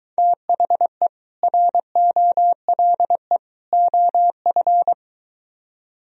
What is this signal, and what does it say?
Morse code, heard as a keyed tone, about 23 wpm, 700 Hz. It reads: THE ROLE OF